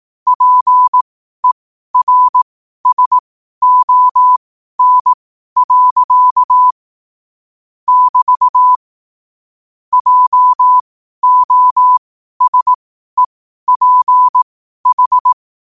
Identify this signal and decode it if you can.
Morse code, heard as a keyed tone, about 18 wpm, spaced as for 12 wpm, 1000 Hz. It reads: PERSON. <BT> JOSEPH
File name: qotdfiles / 12wpm